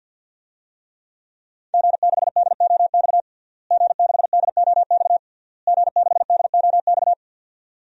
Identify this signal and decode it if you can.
Morse code, heard as a keyed tone, about 52 wpm, 700 Hz. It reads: G6DOX G6DOX G6DOX
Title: DXCC-100 Call Signs - 5 Characters - Repeated 3x - 50wpm